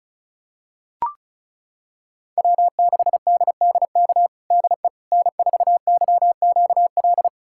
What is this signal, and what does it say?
Morse code, heard as a keyed tone, about 34 wpm, 700 Hz. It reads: W6DDK DE N4YQL
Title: USA Callsign DE Callsign - 5 Characters - 35wpm